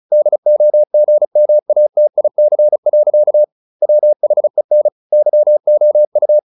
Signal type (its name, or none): Morse code